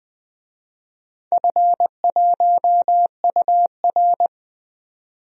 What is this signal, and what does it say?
Morse code, heard as a keyed tone, about 20 wpm, 700 Hz. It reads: F1UR
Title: DXCC-100 Call Signs - 4 Characters - 20wpm